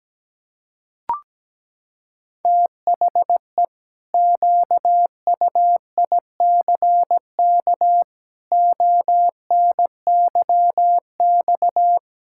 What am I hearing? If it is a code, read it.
Morse code, heard as a keyed tone, about 17 wpm, 700 Hz. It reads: THE QUICK ONYX
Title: Pangram Challenge v2 - 17wpm